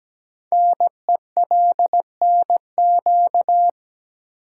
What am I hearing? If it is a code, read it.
Morse code, heard as a keyed tone, about 17 wpm, 700 Hz. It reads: NELNQ
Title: Code Groups - 5 Letters - 17wpm